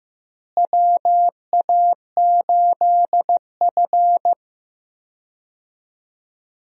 Morse code, heard as a keyed tone, about 15 wpm, 700 Hz. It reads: WA8F
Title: USA Call Signs - 4 Characters - Repeated 3x - 15wpm